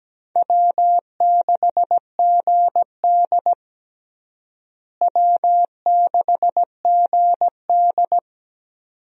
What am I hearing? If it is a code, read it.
Morse code, heard as a keyed tone, about 17 wpm, 700 Hz. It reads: W6GD W6GD
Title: USA Call Signs - 4 Characters - Repeated 2x - 17wpm